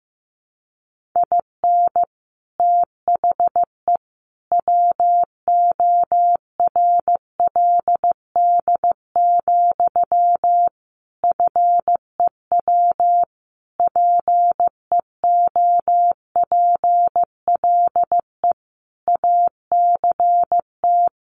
Morse code, heard as a keyed tone, about 15 wpm, 700 Hz. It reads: IN THE WORLD, FEW PEOPLE ACT
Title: Sentences from Top 1400 Words 15wpm